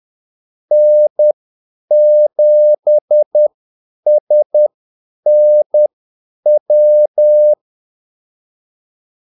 Morse code, heard as a keyed tone, about 10 wpm, spaced as for 8 wpm, 600 Hz. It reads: N7SNW
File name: Callsigns-10-8wpm